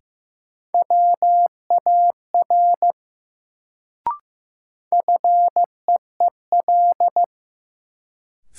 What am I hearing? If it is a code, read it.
Morse code, heard as a keyed tone, about 15 wpm, 700 Hz. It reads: WAR FEEL
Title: Sentences from Top 400 Words - New Word Drill - 15wpm